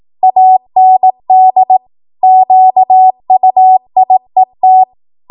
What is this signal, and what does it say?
Morse code, heard as a keyed tone, about 18 wpm, 750 Hz. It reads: AND QUIET